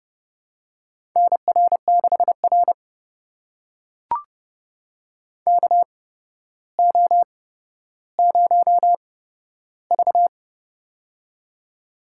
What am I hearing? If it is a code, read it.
Morse code, heard as a keyed tone, about 30 wpm, spaced as for 13 wpm, 700 Hz. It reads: NR6R KO0V
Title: USA Call Signs - 4 Characters - 8x Character Spacing - 30wpm